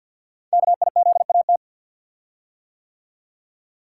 Morse code, heard as a keyed tone, about 52 wpm, 700 Hz. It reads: KIZAT
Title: Code Groups - 5 Letters - 50wpm